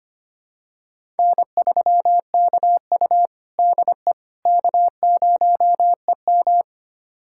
Morse code, heard as a keyed tone, about 25 wpm, 700 Hz. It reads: N3KU DE K0EM
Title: USA Callsign DE Callsign - 4 Characters - 25wpm